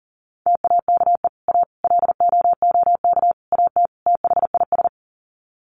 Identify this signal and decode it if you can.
Morse code, heard as a keyed tone, about 37 wpm, 700 Hz. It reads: TAKE A LOOK AT THIS
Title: Sets of 5 Words 40wpm